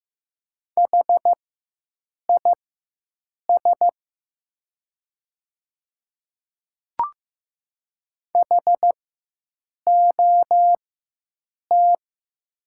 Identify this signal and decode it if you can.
Morse code, heard as a keyed tone, about 15 wpm, spaced as for 7 wpm, 700 Hz. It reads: HIS HOT
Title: 4x Character Spacing - Top 300 Words - 15wpm